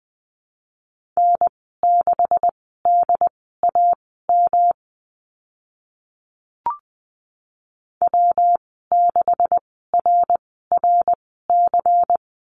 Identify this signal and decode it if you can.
Morse code, heard as a keyed tone, about 20 wpm, spaced as for 15 wpm, 700 Hz. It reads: N6DAM W6RRC
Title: 140 Lesson 23g - Letter f - Call Signs 20wpm